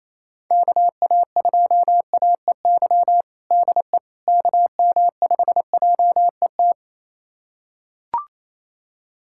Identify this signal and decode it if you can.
Morse code, heard as a keyed tone, about 27 wpm, 700 Hz. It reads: KA2AEY DE KM5JET
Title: USA Callsign DE Callsign - 6 Characters - 28wpm